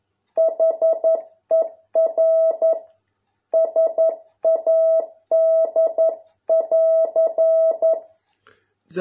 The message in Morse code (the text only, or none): HER SAD<AR>